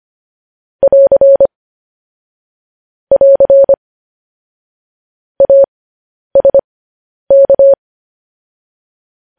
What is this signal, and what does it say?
Morse code, heard as a keyed tone, about 25 wpm, spaced as for 10 wpm, 550 Hz. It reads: <AR> <AR> ASK
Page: news-headlines-in-morse-code-at-10-wpm